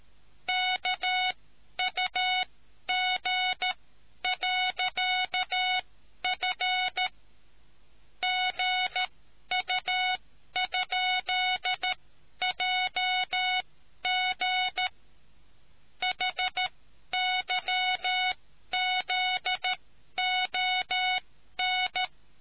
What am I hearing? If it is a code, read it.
Morse code, heard as a keyed tone, about 13 wpm, spaced as for 11 wpm, 740 Hz. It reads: KUG.F GU?JG HYZON